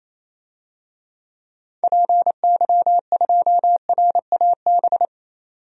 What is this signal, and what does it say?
Morse code, heard as a keyed tone, about 28 wpm, 700 Hz. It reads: PY2RAB
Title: DXCC-100 Call Signs - 6 Characters - 28wpm